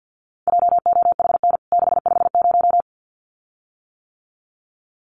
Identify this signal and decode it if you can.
Morse code, heard as a keyed tone, about 49 wpm, 700 Hz. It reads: JOHN 650
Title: CWT Exchange 50wpm